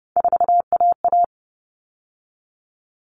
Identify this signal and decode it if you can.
Morse code, heard as a keyed tone, about 30 wpm, 700 Hz. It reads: 4AA
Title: All Cut Numbers - 3-Digit 30wpm